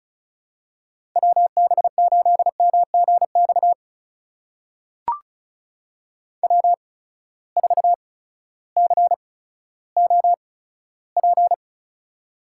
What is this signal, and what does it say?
Morse code, heard as a keyed tone, about 36 wpm, spaced as for 16 wpm, 700 Hz. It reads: WB8MGX W4COP
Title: USA Call Signs - 8x Character Spacing - 35wpm